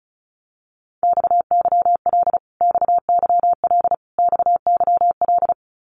Morse code, heard as a keyed tone, about 34 wpm, 700 Hz. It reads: XYL XYL XYL
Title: Abbreviations Repeated 3x - 35wpm